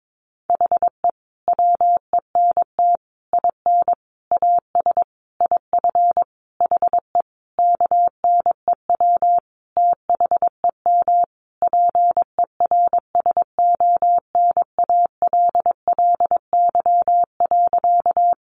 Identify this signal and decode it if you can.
Morse code, heard as a keyed tone, about 22 wpm, 700 Hz. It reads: HE WENT IN AS IF HE KNEW THEM PERSONALLY.
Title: Sentences from Top 1400 Words 22wpm